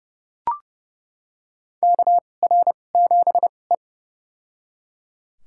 Morse code, heard as a keyed tone, about 30 wpm, spaced as for 22 wpm, 700 Hz. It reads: KR7E
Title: USA Call Signs - 2x Character Spacing - 30wpm